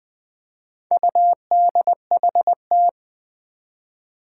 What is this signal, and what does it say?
Morse code, heard as a keyed tone, about 20 wpm, 700 Hz. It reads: UDHT
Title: Four Challenging Letter-Number Combinations 20wpm